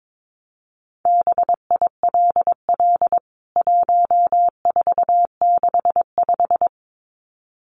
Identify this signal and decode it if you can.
Morse code, heard as a keyed tone, about 22 wpm, 700 Hz. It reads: BILL 1465
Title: CWT Exchange 22wpm